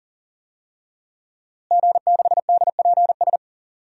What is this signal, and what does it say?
Morse code, heard as a keyed tone, about 40 wpm, 700 Hz. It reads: G6DPS
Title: DXCC-100 Call Signs - 5 Characters - Repeated 2x - 40wpm